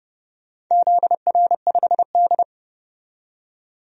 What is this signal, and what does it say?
Morse code, heard as a keyed tone, about 29 wpm, 700 Hz. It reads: ZR5D
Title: DXCC-100 Call Signs - 4 Characters - 30wpm